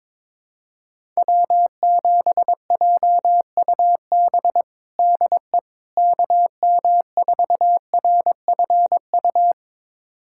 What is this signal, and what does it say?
Morse code, heard as a keyed tone, about 23 wpm, 700 Hz. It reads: W7JUB DE KM4RFU
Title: USA Callsign DE Callsign 22wpm